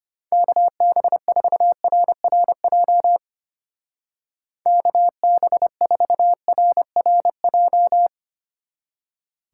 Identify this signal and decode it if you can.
Morse code, heard as a keyed tone, about 31 wpm, 700 Hz. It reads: KB4RRJ KB4RRJ
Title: Speed Racing - USA 2x3 Call Signs 25wpm